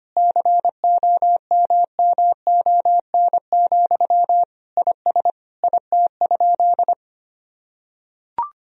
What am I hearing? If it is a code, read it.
Morse code, heard as a keyed tone, about 25 wpm, 700 Hz. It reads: COMMON, IS IT?
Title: Sentences from Top 1000 Words - Review Part 3 - 25wpm